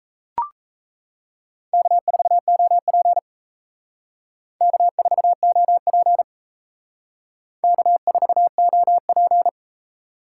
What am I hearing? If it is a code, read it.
Morse code, heard as a keyed tone, about 35 wpm, 700 Hz. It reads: K4OP K4OP K4OP
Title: Speed Racing - USA Call Signs 28wpm